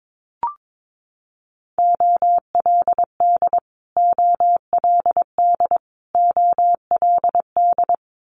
Morse code, heard as a keyed tone, about 22 wpm, 700 Hz. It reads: OLD OLD OLD
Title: Multi-letter Phonemes Repeated 3x 22wpm